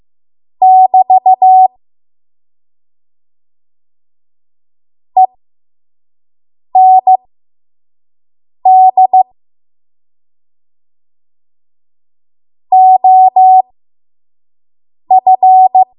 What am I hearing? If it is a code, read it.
Morse code, heard as a keyed tone, about 15 wpm, spaced as for 5 wpm, 750 Hz. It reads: <BT> END OF
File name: Morse / 05WPM